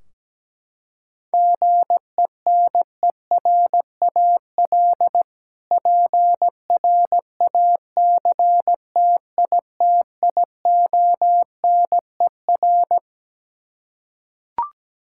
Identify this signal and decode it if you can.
Morse code, heard as a keyed tone, about 17 wpm, 700 Hz. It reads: GENERAL PRACTITIONER
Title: Practice QSO - Exchange 3 - More Info - 17wpm